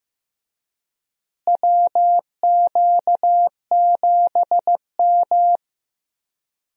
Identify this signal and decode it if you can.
Morse code, heard as a keyed tone, about 16 wpm, 700 Hz. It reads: WQ7M
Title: USA Call Signs - 4 Characters - Repeated 3x - 15wpm